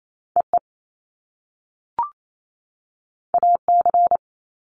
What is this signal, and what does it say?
Morse code, heard as a keyed tone, about 28 wpm, 700 Hz. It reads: EE AC
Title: Top Two Letter Combinations in Words 28wpm